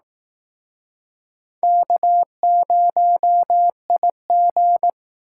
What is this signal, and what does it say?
Morse code, heard as a keyed tone, about 20 wpm, 700 Hz. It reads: K0IG